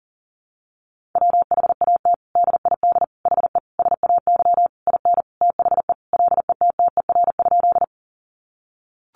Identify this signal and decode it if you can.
Morse code, heard as a keyed tone, about 40 wpm, 700 Hz. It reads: WHAT DID HE SAY IN THE LETTER?
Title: Sentences from Top 300 Words 40wpm